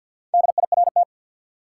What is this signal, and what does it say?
Morse code, heard as a keyed tone, about 47 wpm, 700 Hz. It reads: DIRT
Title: Four Letter Words - 50wpm